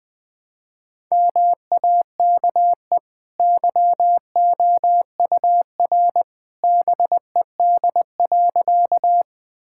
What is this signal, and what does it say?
Morse code, heard as a keyed tone, about 21 wpm, 700 Hz. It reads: MAKE YOUR BED.
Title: Easy Sentences with Hints - 20wpm